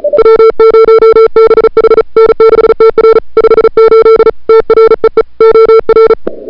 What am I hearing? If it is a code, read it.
Morse code, heard as a keyed tone, about 35 wpm, 440 Hz. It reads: W0BH N6TR 58 TREE OR
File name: n6tr / Fab2007CWSprintHour1